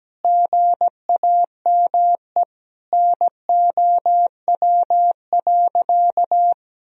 Morse code, heard as a keyed tone, about 17 wpm, 700 Hz. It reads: GAME NOW.
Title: Sentences from Top 1000 Words - Review Part 3 - 17wpm